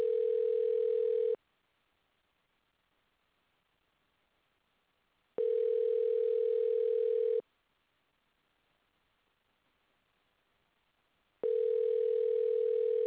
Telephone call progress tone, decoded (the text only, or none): ringback tone